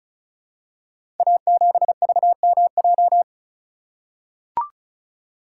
Morse code, heard as a keyed tone, about 36 wpm, 700 Hz. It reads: A7VMJ
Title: Code Groups - 5 Letter:Numbers - 35wpm